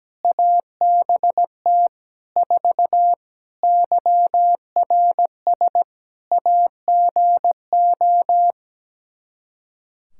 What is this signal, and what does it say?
Morse code, heard as a keyed tone, about 17 wpm, 700 Hz. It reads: ABT 4 YRS AGO
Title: Practice QSO - Exchange 3 - More Info - 17wpm